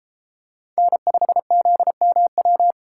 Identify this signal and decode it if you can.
Morse code, heard as a keyed tone, about 30 wpm, 700 Hz. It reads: N5ZMW